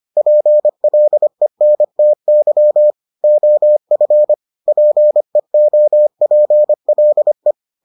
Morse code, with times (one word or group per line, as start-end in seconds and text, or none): 0.2-2.9 PLENTY
3.2-4.3 OF
4.7-7.5 PEOPLE